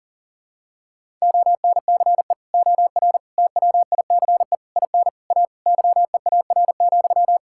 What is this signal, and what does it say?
Morse code, heard as a keyed tone, about 37 wpm, 700 Hz. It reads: ONCE OR TWICE IN A YEAR,
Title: Sentences from Top 1600 Words 40wpm